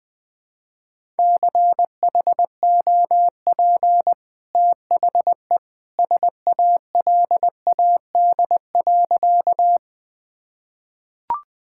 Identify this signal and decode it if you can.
Morse code, heard as a keyed tone, about 20 wpm, 700 Hz. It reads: CHOP THE SALAD.
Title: Easy Sentences with Hints - 20wpm